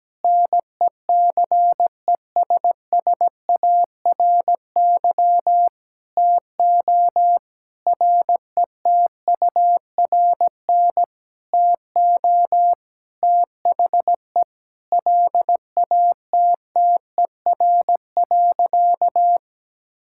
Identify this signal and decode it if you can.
Morse code, heard as a keyed tone, about 17 wpm, 700 Hz. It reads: NECESSARY TO RETURN TO THE LATTER.
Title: Sentences from Top 1600 Words 17wpm